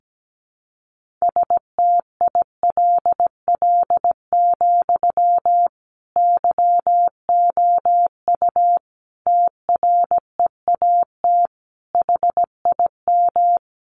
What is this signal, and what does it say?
Morse code, heard as a keyed tone, about 17 wpm, 700 Hz. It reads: STILL, YOU TREAT HIM
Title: Sentences from Top 1300 Words 17wpm